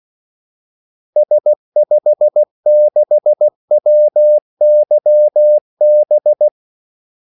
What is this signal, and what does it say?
Morse code, heard as a keyed tone, about 16 wpm, 600 Hz. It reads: S56WYB
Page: Kaldesignaler 16 wpm